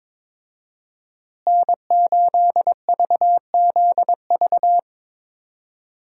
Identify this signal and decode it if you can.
Morse code, heard as a keyed tone, about 22 wpm, 700 Hz. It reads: N8VZV
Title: USA Call Signs - 5 Characters - Repeated 3x - 22wpm